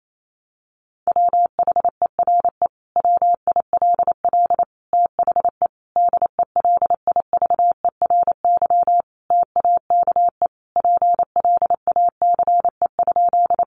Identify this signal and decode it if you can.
Morse code, heard as a keyed tone, about 28 wpm, 700 Hz. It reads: WHERE WILL THE DELIVERY TAKE PLACE?
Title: Sentences from Top 1500 Words 28wpm